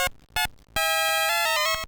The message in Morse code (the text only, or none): none